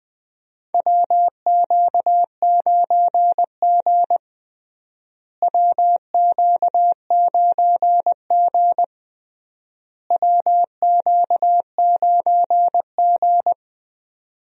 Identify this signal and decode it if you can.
Morse code, heard as a keyed tone, about 20 wpm, 700 Hz. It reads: WQ9G WQ9G WQ9G
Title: USA Call Signs - 4 Characters - Repeated 3x - 20wpm